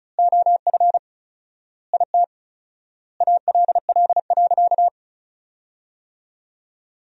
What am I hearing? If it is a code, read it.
Morse code, heard as a keyed tone, about 34 wpm, 700 Hz. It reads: OF IT ALL.
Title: 4x Word-Spacing - Sentences from Top 200 Words - 35wpm